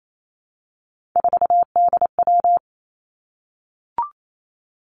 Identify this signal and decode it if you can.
Morse code, heard as a keyed tone, about 28 wpm, 700 Hz. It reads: 4DW
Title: Three Challenging Letter-Number Combinations 28wpm